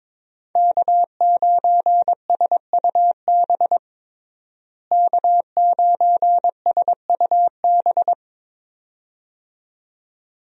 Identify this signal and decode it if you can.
Morse code, heard as a keyed tone, about 22 wpm, 700 Hz. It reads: K9SUB K9SUB
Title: USA Call Signs - 5 Characters - Repeated 2x - 22wpm